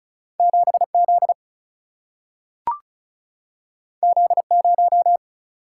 Morse code, heard as a keyed tone, about 35 wpm, 700 Hz. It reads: 7Z Z0
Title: Single-Letter Single-Number 35wpm